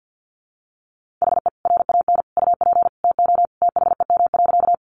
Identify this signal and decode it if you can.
Morse code, heard as a keyed tone, about 50 wpm, 700 Hz. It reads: HE RAN UP TO THEM.
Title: Sentences from Top 500 Words 50wpm